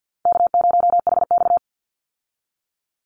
Morse code, heard as a keyed tone, about 50 wpm, 700 Hz. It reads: K0HX
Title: Call Signs Repeated 2x - 50wpm